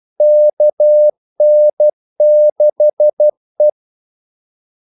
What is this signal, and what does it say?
Morse code, heard as a keyed tone, about 12 wpm, 600 Hz. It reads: KN6E
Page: 2 ekstra ordmellemrum mellem kaldesignalerne